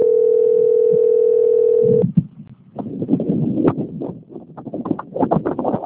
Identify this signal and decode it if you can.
Ringback tone (North America), 440 + 480 Hz.